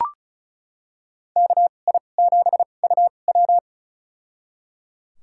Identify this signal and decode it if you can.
Morse code, heard as a keyed tone, about 36 wpm, spaced as for 25 wpm, 700 Hz. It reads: KI7UW
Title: USA Call Signs - 5 Characters - 2x Character Spacing - 35wpm